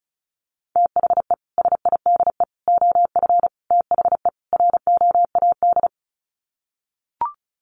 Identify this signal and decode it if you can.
Morse code, heard as a keyed tone, about 34 wpm, 700 Hz. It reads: THE SIDE OF THE ROAD
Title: Sets of 5 Words 35wpm